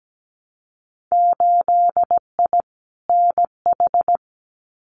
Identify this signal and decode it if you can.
Morse code, heard as a keyed tone, about 17 wpm, 700 Hz. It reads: 8I NH
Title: Winter Field Day Exchanges Encore 17wpm